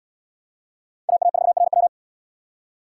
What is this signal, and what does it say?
Morse code, heard as a keyed tone, about 78 wpm, 700 Hz. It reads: AA4RW